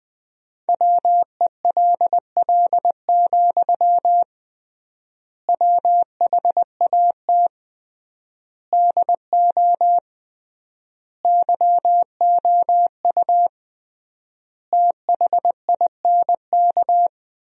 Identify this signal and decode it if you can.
Morse code, heard as a keyed tone, about 20 wpm, 700 Hz. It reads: WELL, WHAT DO YOU THINK
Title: 3x Word-Spacing - Sentences from Top 200 Words - 20wpm